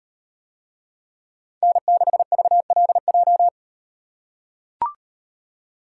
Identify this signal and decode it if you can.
Morse code, heard as a keyed tone, about 36 wpm, 700 Hz. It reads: N6VLJ